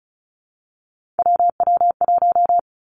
Morse code, heard as a keyed tone, about 36 wpm, 700 Hz. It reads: WW1
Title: Three Challenging Letter-Number Combinations 35wpm